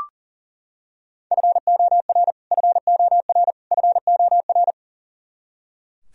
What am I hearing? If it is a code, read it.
Morse code, heard as a keyed tone, about 38 wpm, 700 Hz. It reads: FOR FOR FOR
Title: Top 100 Words Repeated 40wpm